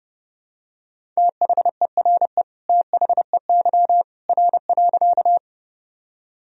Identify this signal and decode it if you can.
Morse code, heard as a keyed tone, about 30 wpm, 700 Hz. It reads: THERE THEY R.
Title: Sentences from Top 100 Words and Abbreviations 30wpm